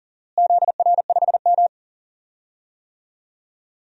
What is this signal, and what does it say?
Morse code, heard as a keyed tone, about 39 wpm, 700 Hz. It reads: ZR5M
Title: DXCC-100 Call Signs - 4 Characters - Repeated 3x - 40wpm